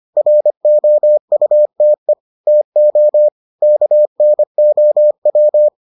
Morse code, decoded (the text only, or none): ROUTE TO KNOW